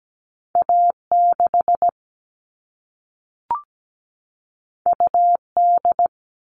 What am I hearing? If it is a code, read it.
Morse code, heard as a keyed tone, about 17 wpm, 700 Hz. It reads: A6 UD
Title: All Cut Numbers - 2-Digit 17wpm